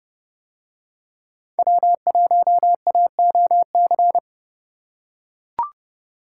Morse code, heard as a keyed tone, about 30 wpm, 700 Hz. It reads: W1AOC